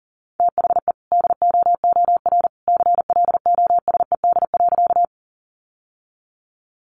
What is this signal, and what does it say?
Morse code, heard as a keyed tone, about 40 wpm, 700 Hz. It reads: THE DOOR CLOSED.
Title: Sentences from Top 1400 Words 40wpm